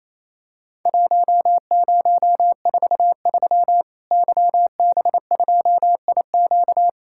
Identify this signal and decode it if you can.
Morse code, heard as a keyed tone, about 28 wpm, 700 Hz. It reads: 1043 YB2IQ
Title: Worked All Europe DX Contest - QTC Exchange - 28wpm